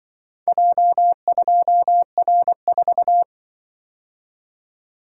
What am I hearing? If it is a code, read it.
Morse code, heard as a keyed tone, about 25 wpm, 700 Hz. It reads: J2R4